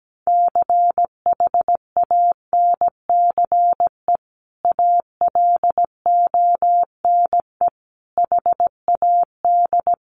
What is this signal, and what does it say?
Morse code, heard as a keyed tone, about 17 wpm, 700 Hz. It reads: CHANCE ALONE HAD
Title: Sentences from Top 1000 Words 17wpm